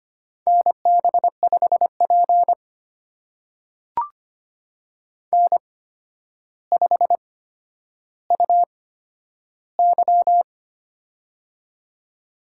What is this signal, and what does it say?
Morse code, heard as a keyed tone, about 25 wpm, spaced as for 11 wpm, 700 Hz. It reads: NB5P N5UY
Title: USA Call Signs - 4 Characters - 8x Character Spacing - 25wpm